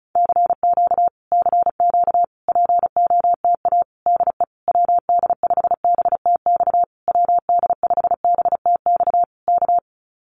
Morse code, heard as a keyed tone, about 34 wpm, 700 Hz. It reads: CQ CQ POTA DE WD5BTX WD5BTX K
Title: Common QSO Phrases Encore 35wpm